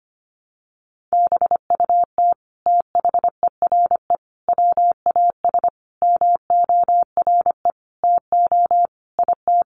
Morse code, heard as a keyed tone, about 25 wpm, 700 Hz. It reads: BUT THERE WAS MORE TO IT
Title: Sentences from Top 100 Words 25wpm